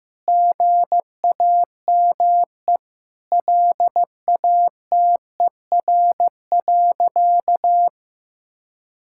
Morse code, heard as a keyed tone, about 15 wpm, 700 Hz. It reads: GAME LATER.
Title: Sentences from Top 1000 Words - Review Part 6 - 15wpm